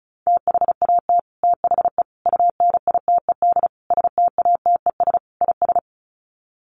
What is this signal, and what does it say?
Morse code, heard as a keyed tone, about 34 wpm, 700 Hz. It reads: THAT THE UNITED STATES IS
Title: Sets of 5 Words Encore 35wpm